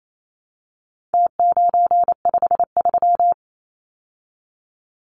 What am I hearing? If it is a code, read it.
Morse code, heard as a keyed tone, about 28 wpm, 700 Hz. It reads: T953
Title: Numbers - Contest Style - 4-Digit 28wpm